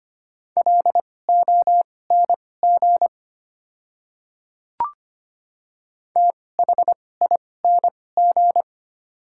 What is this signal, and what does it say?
Morse code, heard as a keyed tone, about 25 wpm, spaced as for 18 wpm, 700 Hz. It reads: LONG THING
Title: 2x Character Spacing - Top 500 Words - 25wpm